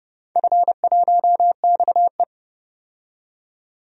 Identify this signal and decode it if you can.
Morse code, heard as a keyed tone, about 29 wpm, 700 Hz. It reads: F1XE
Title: DXCC-100 Call Signs - 4 Characters - 30wpm